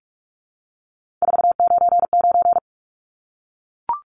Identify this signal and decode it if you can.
Morse code, heard as a keyed tone, about 44 wpm, 700 Hz. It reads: 499
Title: Numbers - 3-Digit 45wpm